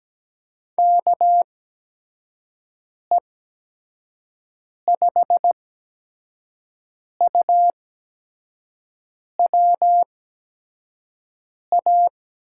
Morse code, heard as a keyed tone, about 17 wpm, spaced as for 5 wpm, 700 Hz. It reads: KE5UWA
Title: USA Call Signs - 6 Characters - 8x Character Spacing - 17wpm